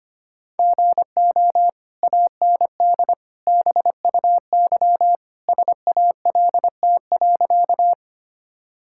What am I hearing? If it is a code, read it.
Morse code, heard as a keyed tone, about 25 wpm, 700 Hz. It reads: GO AND BUY SALT.
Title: Sentences from Top 1000 Words - Review Part 5 - 25wpm